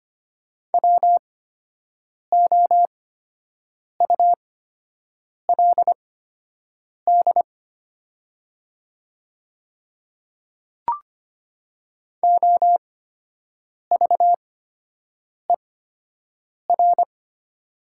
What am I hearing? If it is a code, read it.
Morse code, heard as a keyed tone, about 25 wpm, spaced as for 7 wpm, 700 Hz. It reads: WOULD OVER
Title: 8x Character Spacing - Top 100 Words - 25wpm